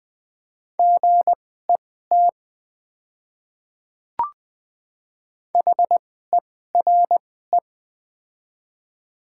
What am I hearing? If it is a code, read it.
Morse code, heard as a keyed tone, about 20 wpm, spaced as for 14 wpm, 700 Hz. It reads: GET HERE
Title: 2x Character Spacing - Top 200 Words - 20wpm